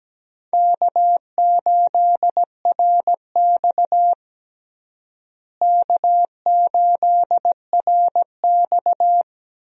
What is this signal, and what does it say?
Morse code, heard as a keyed tone, about 17 wpm, 700 Hz. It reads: K8RX K8RX
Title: USA Call Signs - 4 Characters - Repeated 2x - 17wpm